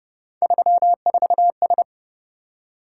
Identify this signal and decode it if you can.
Morse code, heard as a keyed tone, about 31 wpm, 700 Hz. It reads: 34S
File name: Speed-Racing - Three Letter-Number Combinations - 25wpm